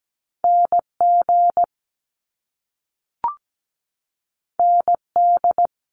Morse code, heard as a keyed tone, about 17 wpm, 700 Hz. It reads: NG ND
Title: Top Two Letter Combinations in Words 17wpm